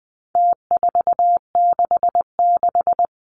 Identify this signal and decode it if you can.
Morse code, heard as a keyed tone, about 20 wpm, 700 Hz. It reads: T466
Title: Numbers - Contest Style - 4-Digit 20wpm